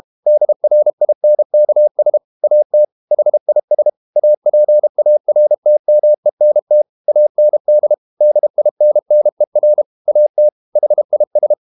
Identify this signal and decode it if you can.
Morse code, heard as a keyed tone, about 32 wpm, 600 Hz. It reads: DRINKS AT HIS APARTMENT AND DINNER AT HIS
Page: Never 31wpm | CW med Gnister